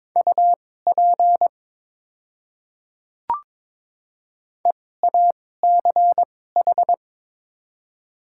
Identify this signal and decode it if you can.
Morse code, heard as a keyed tone, about 22 wpm, spaced as for 16 wpm, 700 Hz. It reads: UP EACH
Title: 2x Character Spacing - Top 100 Words - 22wpm